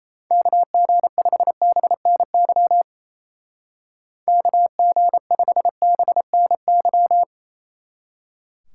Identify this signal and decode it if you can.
Morse code, heard as a keyed tone, about 35 wpm, 700 Hz. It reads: KG5BNY KG5BNY
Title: Speed Racing - USA 2x3 Call Signs 28wpm